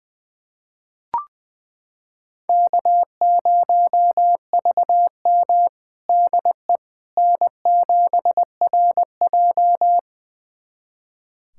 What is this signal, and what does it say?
Morse code, heard as a keyed tone, about 20 wpm, 700 Hz. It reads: K0VM DE N7RJ
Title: USA Callsign DE Callsign - 4 Characters - 20wpm